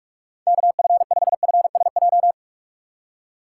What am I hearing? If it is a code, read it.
Morse code, heard as a keyed tone, about 57 wpm, 700 Hz. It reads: KF5FSJ